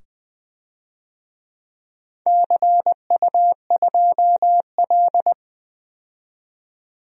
Morse code, heard as a keyed tone, about 20 wpm, 700 Hz. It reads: CU2L
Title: DXCC-100 Call Signs - 4 Characters - 20wpm